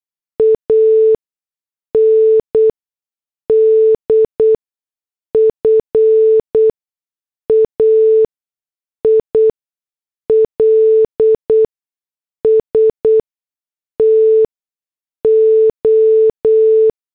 Morse code, heard as a keyed tone, about 8 wpm, 440 Hz. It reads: A N D F A I L S T O
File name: morse / headlines